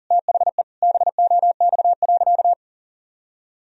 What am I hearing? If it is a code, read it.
Morse code, heard as a keyed tone, about 37 wpm, 700 Hz. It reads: THE BOX.
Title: Sentences from Top 1000 Words - Review Part 2 - 40wpm